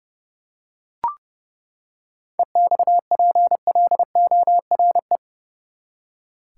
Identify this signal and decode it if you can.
Morse code, heard as a keyed tone, about 32 wpm, 700 Hz. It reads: EXPLORE
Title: Seven Letter Words - 30wpm